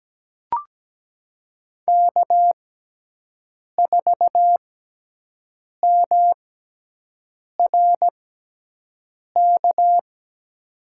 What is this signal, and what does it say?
Morse code, heard as a keyed tone, about 17 wpm, spaced as for 6 wpm, 700 Hz. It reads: K4MRK
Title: USA Call Signs - 6x Character Spacing - 17wpm